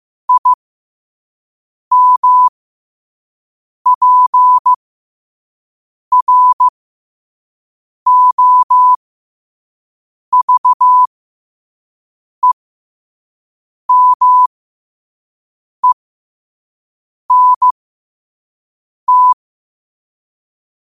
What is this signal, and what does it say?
Morse code, heard as a keyed tone, about 15 wpm, spaced as for 5 wpm, 1000 Hz. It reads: IMPROVEMENT
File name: qotdfiles / 5wpm